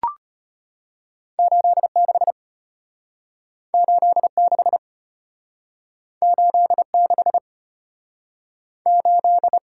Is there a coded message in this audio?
Yes: Morse code from 1.4 to 9.6 s, heard as a keyed tone, about 32 wpm, 700 Hz. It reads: 86 86 86 8